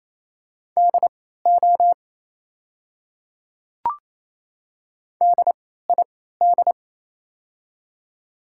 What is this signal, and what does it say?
Morse code, heard as a keyed tone, about 28 wpm, spaced as for 16 wpm, 690 Hz. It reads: DO DID
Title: 3x Character Spacing - Top 100 Words - 28wpm